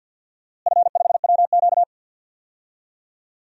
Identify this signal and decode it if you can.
Morse code, heard as a keyed tone, about 48 wpm, 700 Hz. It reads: F5WQ
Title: DXCC-100 Call Signs - 4 Characters - Repeated 2x - 50wpm